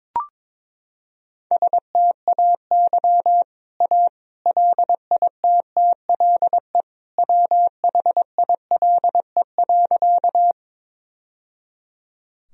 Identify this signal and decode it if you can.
Morse code, heard as a keyed tone, about 22 wpm, 700 Hz. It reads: STAY A LITTLE WHILE.
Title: Sentences from Top 1000 Words - Review Part 1 - 22wpm